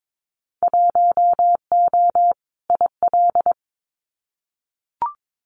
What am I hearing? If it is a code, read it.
Morse code, heard as a keyed tone, about 22 wpm, 700 Hz. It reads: 1O IL
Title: Winter Field Day Exchanges 22wpm